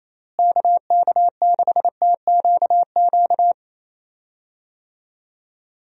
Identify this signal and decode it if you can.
Morse code, heard as a keyed tone, about 28 wpm, 700 Hz. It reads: KK6TQQ
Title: USA Call Signs - 6 Characters - Repeated 3x - 28wpm